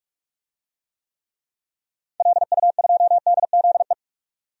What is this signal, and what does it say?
Morse code, heard as a keyed tone, about 44 wpm, 700 Hz. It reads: LU2DZE
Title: DXCC-100 Call Signs - 6 Characters - 45wpm